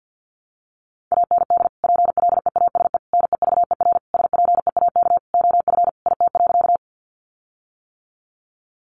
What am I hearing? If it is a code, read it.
Morse code, heard as a keyed tone, about 50 wpm, 700 Hz. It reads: AND PLEASE NEVER SPEAK OF IT.
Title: Sentences from Top 1500 Words 50wpm